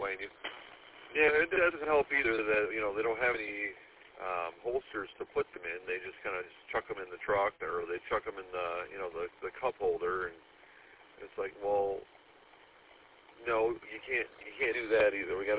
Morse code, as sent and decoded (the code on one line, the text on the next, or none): none